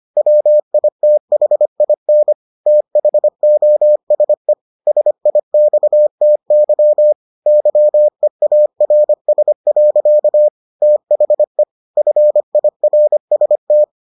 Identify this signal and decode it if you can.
Morse code, heard as a keyed tone, about 25 wpm, 600 Hz. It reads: WITHIN THOSE SIXTY YEARS. THE FIRST